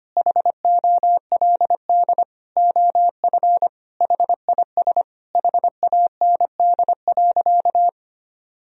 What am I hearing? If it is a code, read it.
Morse code, heard as a keyed tone, about 25 wpm, 700 Hz. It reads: HOLD OF HIS HAND.
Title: Sentences from Top 1000 Words - Review Part 2 - 25wpm